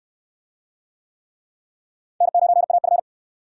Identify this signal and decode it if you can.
Morse code, heard as a keyed tone, about 64 wpm, 700 Hz. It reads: N9AF